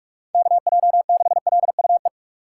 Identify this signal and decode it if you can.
Morse code, heard as a keyed tone, about 43 wpm, 700 Hz. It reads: KJ6LUE
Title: USA Call Signs - 6 Characters - 4x Character Spacing - 45wpm